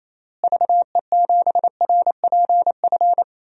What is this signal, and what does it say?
Morse code, heard as a keyed tone, about 28 wpm, 700 Hz. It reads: VE7RPF
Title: DXCC-100 Call Signs - 6 Characters - Repeated 2x - 28wpm